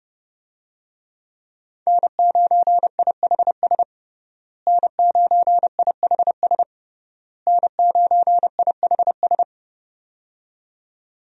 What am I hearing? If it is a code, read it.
Morse code, heard as a keyed tone, about 32 wpm, 700 Hz. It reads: N9IHS N9IHS N9IHS
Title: USA Call Signs - 5 Characters - Repeated 3x - 30wpm